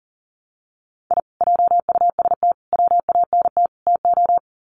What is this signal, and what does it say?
Morse code, heard as a keyed tone, about 40 wpm, 700 Hz. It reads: I JUST WANT TO
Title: Sets of 4 Words 40wpm